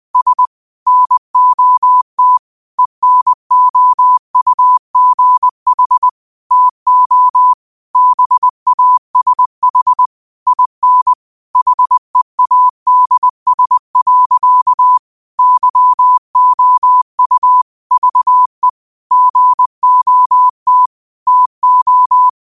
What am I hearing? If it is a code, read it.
Morse code, heard as a keyed tone, about 20 wpm, 1000 Hz. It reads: S NOT ENOUGH TO BASH IN HEADS. YOU VE GOT TO
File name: qotdfiles / 20wpm